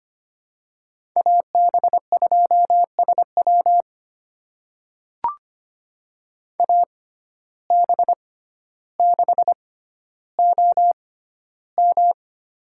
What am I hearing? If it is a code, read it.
Morse code, heard as a keyed tone, about 25 wpm, spaced as for 13 wpm, 700 Hz. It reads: AB2SW AB6OM
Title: USA Call Signs - 5 Characters - 6x Character Spacing - 25wpm